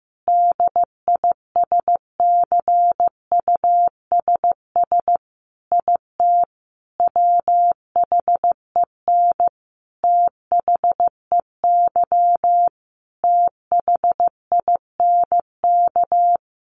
Morse code, heard as a keyed tone, about 15 wpm, 700 Hz. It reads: DISCUSS IT WHEN THEY THINK
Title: Sentences from Top 1400 Words 15wpm